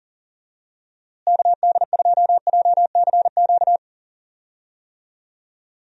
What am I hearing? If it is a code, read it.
Morse code, heard as a keyed tone, about 40 wpm, 700 Hz. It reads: KD2JCQ
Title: USA Call Signs - 6 Characters - Repeated 3x - 40wpm